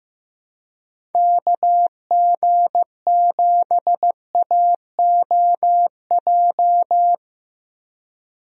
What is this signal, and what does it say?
Morse code, heard as a keyed tone, about 15 wpm, 700 Hz. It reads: KG7AOJ
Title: USA Call Signs - 6 Characters - Repeated 3x - 15wpm